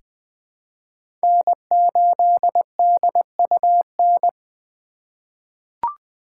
Morse code, heard as a keyed tone, about 20 wpm, 700 Hz. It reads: N8DUN
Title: USA Call Signs - 5 Characters - 4x Character Spacing - 20wpm